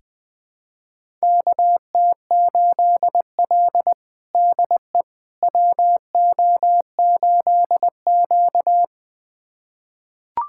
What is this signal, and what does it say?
Morse code, heard as a keyed tone, about 20 wpm, 700 Hz. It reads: KT8L DE WO8Q
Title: USA Callsign DE Callsign - 4 Characters - 20wpm